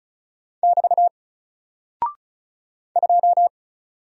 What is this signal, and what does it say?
Morse code, heard as a keyed tone, about 36 wpm, 700 Hz. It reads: <BT> 2
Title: Single Character (Letter:Number:Punctuation:Prosign) 35wpm